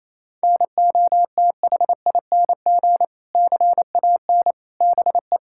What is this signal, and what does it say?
Morse code, heard as a keyed tone, about 27 wpm, 700 Hz. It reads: NOTHING CAN BE
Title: Sentences from Top 1000 Words - Review Part 6 - 28wpm